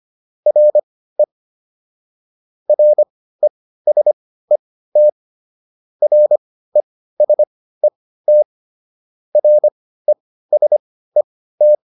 Morse code, heard as a keyed tone, about 25 wpm, spaced as for 15 wpm, 600 Hz. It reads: RE RESET RESET RESET